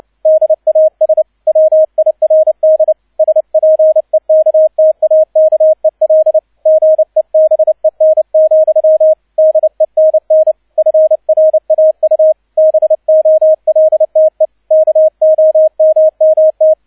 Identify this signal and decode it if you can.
Morse code, heard as a keyed tone, about 29 wpm, 630 Hz, with radio noise. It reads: DAS WIRD SPEKTAKEL GEBEN, DENN FRAU BOLTE KOMMT
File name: 002 Zweiter Streich 30wpm